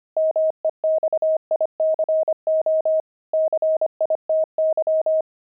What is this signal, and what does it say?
Morse code, heard as a keyed tone, about 25 wpm, 630 Hz. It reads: MEXICO CITY
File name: pangram-fs-25-cs-25-kidding